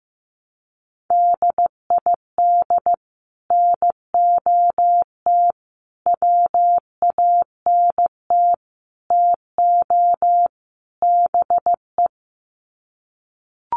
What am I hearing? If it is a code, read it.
Morse code, heard as a keyed tone, about 15 wpm, 700 Hz. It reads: DID NOT WANT TO BE
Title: Sets of 5 Words Encore 15wpm